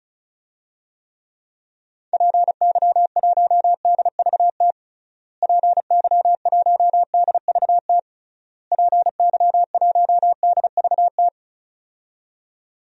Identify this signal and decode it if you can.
Morse code, heard as a keyed tone, about 34 wpm, 700 Hz. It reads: PY1DVT PY1DVT PY1DVT
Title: DXCC-100 Call Signs - 6 Characters - Repeated 3x - 35wpm